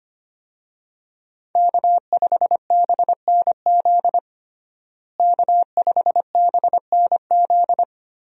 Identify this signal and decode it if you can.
Morse code, heard as a keyed tone, about 25 wpm, 700 Hz. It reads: K5BNZ K5BNZ
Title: USA Call Signs - 5 Characters - Repeated 2x - 25wpm